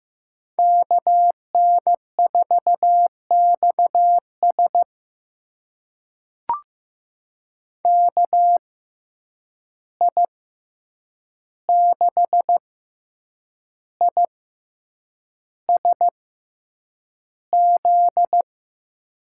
Morse code, heard as a keyed tone, about 15 wpm, spaced as for 7 wpm, 700 Hz. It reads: KN4XS KI6ISZ
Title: USA Call Signs - 6x Character Spacing - 15wpm